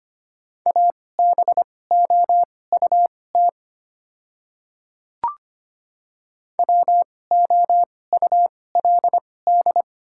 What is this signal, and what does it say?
Morse code, heard as a keyed tone, about 25 wpm, spaced as for 18 wpm, 690 Hz. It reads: ABOUT WOULD
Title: 2x Character Spacing - Top 200 Words - 25wpm